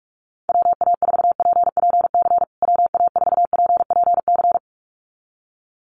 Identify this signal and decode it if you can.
Morse code, heard as a keyed tone, about 44 wpm, 700 Hz. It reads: WA4PPC WA4PPC
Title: USA Call Signs Repeated 2x 45wpm